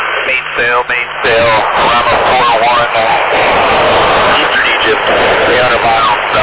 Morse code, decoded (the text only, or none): none